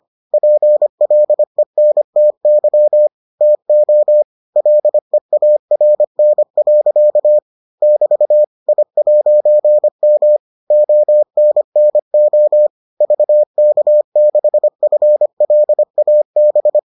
Morse code, heard as a keyed tone, about 25 wpm, 600 Hz. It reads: PLENTY TO LEARN. <BT> I'M ONNO VK6FLAB